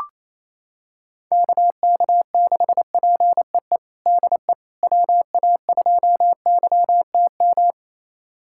Morse code, heard as a keyed tone, about 28 wpm, 700 Hz. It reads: KK6PEE DE WA2YTM
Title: USA Callsign DE Callsign - 6 Characters - 28wpm